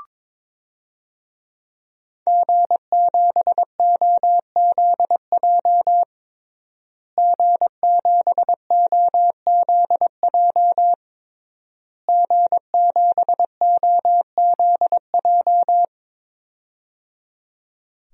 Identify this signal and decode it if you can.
Morse code, heard as a keyed tone, about 22 wpm, 700 Hz. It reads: G7OZJ G7OZJ G7OZJ
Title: DXCC-100 Call Signs - 5 Characters - Repeated 3x - 22wpm